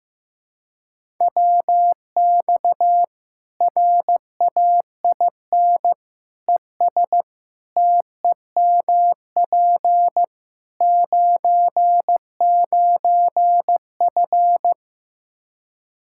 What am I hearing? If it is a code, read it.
Morse code, heard as a keyed tone, about 16 wpm, 700 Hz. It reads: WX RAIN ES TEMP 99F
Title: Practice QSO - Exchange 2 - Info - 15wpm